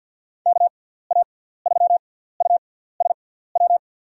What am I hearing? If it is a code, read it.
Morse code, heard as a keyed tone, about 50 wpm, spaced as for 17 wpm, 700 Hz. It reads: KA3USW
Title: USA Call Signs - 6 Characters - 6x Character Spacing - 50wpm